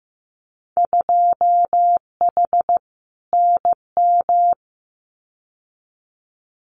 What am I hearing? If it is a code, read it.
Morse code, heard as a keyed tone, about 15 wpm, 700 Hz. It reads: 2H NM
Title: Winter Field Day Exchanges Encore 15wpm